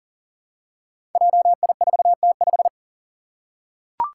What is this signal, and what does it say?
Morse code, heard as a keyed tone, about 40 wpm, 700 Hz. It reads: JI4T5
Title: Five Challenging Letter-Number Combinations 40wpm